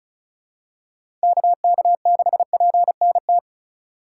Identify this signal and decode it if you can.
Morse code, heard as a keyed tone, about 34 wpm, 700 Hz. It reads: KK6PNT
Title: USA Call Signs - 6 Characters - 2x Character Spacing - 35wpm